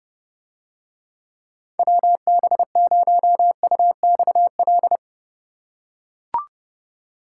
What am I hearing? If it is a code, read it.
Morse code, heard as a keyed tone, about 30 wpm, 700 Hz. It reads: WB0UXL